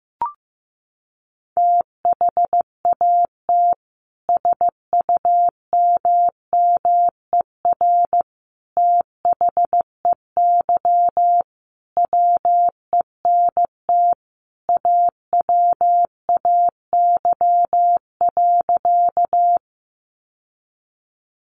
Morse code, heard as a keyed tone, about 15 wpm, 700 Hz. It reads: THAT SUMMER THEY WENT AWAY.
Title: Sentences from Top 1100 Words 15wpm